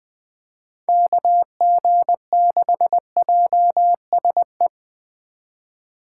Morse code, heard as a keyed tone, about 20 wpm, 700 Hz. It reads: KG6JSE
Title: USA Call Signs - 6 Characters - 4x Character Spacing - 20wpm